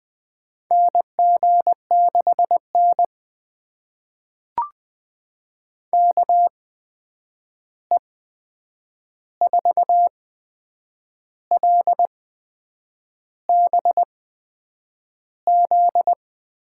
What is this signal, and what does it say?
Morse code, heard as a keyed tone, about 20 wpm, spaced as for 8 wpm, 700 Hz. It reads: NG6N KE4LBZ
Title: USA Call Signs - 8x Character Spacing - 20wpm